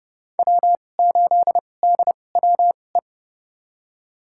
Morse code, heard as a keyed tone, about 30 wpm, spaced as for 22 wpm, 700 Hz. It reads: W8DWE
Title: USA Call Signs - 2x Character Spacing - 30wpm